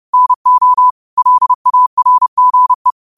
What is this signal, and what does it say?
Morse code, heard as a keyed tone, about 30 wpm, 1000 Hz. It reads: NO LARGE
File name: qotdfiles / 30wpm